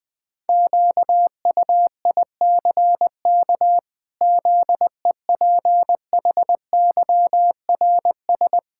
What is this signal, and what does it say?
Morse code, heard as a keyed tone, about 20 wpm, 700 Hz. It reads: QUICK ZEPHYRS
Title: Pangram Challenge v2 - 20wpm